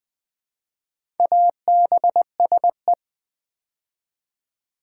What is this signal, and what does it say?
Morse code, heard as a keyed tone, about 20 wpm, 700 Hz. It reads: ABSE
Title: Build A Word - 2 Letters At A Time - 6-Letter Words - 20wpm